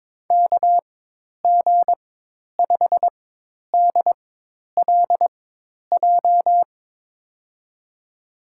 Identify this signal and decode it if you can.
Morse code, heard as a keyed tone, about 22 wpm, spaced as for 10 wpm, 700 Hz. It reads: KG5DLJ
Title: USA Call Signs - 6 Characters - 4x Character Spacing - 22wpm